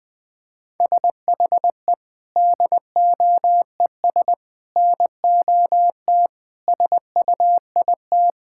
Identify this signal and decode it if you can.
Morse code, heard as a keyed tone, about 21 wpm, 700 Hz. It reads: SHE DOES NOT SUIT
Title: Sentences from Top 1600 Words 20wpm